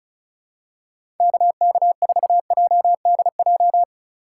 Morse code, heard as a keyed tone, about 34 wpm, 700 Hz. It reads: KK4JDJ
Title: USA Call Signs - 4x Character Spacing - 35wpm